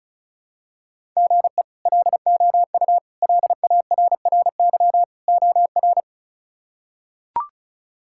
Morse code, heard as a keyed tone, about 36 wpm, 700 Hz. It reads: GE LOU LARRY OR
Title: K1USN SST QSO Practice - 35wpm